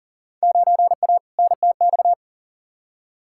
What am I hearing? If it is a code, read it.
Morse code, heard as a keyed tone, about 40 wpm, 700 Hz. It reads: 9A NTX
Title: Field Day Exchanges Encore - 40wpm